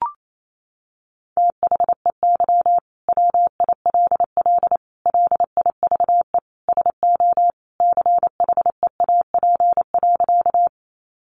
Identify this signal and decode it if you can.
Morse code, heard as a keyed tone, about 28 wpm, 700 Hz. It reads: THEY WILL LIVE SO CHEAP.
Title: Sentences from Top 1200 Words 28wpm